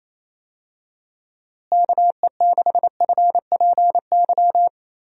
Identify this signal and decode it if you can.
Morse code, heard as a keyed tone, about 28 wpm, 700 Hz. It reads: KE6FPY
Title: USA Call Signs - 6 Characters - Repeated 3x - 28wpm